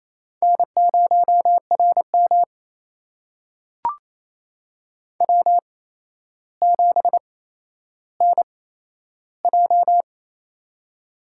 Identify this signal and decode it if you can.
Morse code, heard as a keyed tone, about 28 wpm, spaced as for 12 wpm, 700 Hz. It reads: N0RM W7NJ
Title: USA Call Signs - 4 Characters - 8x Character Spacing - 28wpm